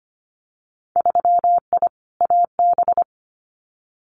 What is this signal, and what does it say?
Morse code, heard as a keyed tone, about 25 wpm, 700 Hz. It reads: 3I AB
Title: Winter Field Day Exchanges Encore 25wpm